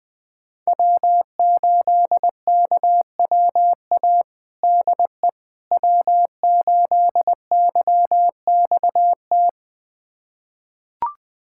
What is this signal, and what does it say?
Morse code, heard as a keyed tone, about 20 wpm, 700 Hz. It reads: W8KWA DE W8YXT
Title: USA Callsign DE Callsign - 5 Characters - 20wpm